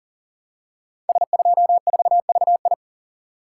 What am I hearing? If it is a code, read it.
Morse code, heard as a keyed tone, about 39 wpm, 700 Hz. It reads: S24VI
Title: Code Groups - 5 Letter:Numbers - 40wpm